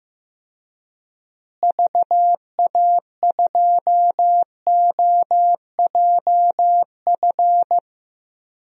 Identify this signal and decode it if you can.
Morse code, heard as a keyed tone, about 15 wpm, 700 Hz. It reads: VA2OJF
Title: DXCC-100 Call Signs - 6 Characters - Repeated 3x - 15wpm